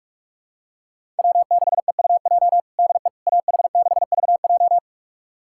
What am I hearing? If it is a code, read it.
Morse code, heard as a keyed tone, about 46 wpm, 700 Hz. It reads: W6EUJ DE AH6VJ
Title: USA Callsign DE Callsign - 5 Characters - 45wpm